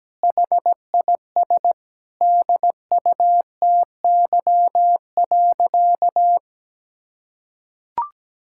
Morse code, heard as a keyed tone, about 17 wpm, 700 Hz. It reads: HIS DUTY.
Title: Sentences from Top 1600 Words 17wpm